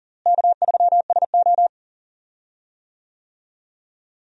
Morse code, heard as a keyed tone, about 39 wpm, 700 Hz. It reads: K3SO
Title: USA Call Signs - 4 Characters - Repeated 3x - 40wpm